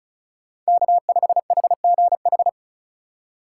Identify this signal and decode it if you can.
Morse code, heard as a keyed tone, about 47 wpm, spaced as for 41 wpm, 700 Hz. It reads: K5HGH
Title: Speed Racing - USA Call Signs 35wpm